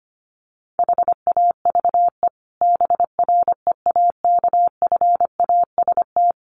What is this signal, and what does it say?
Morse code, heard as a keyed tone, about 25 wpm, 700 Hz. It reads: HAVE BREAKFAST
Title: Sentences from Top 1500 Words 25wpm